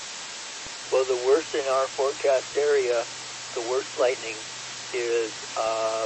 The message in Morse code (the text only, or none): none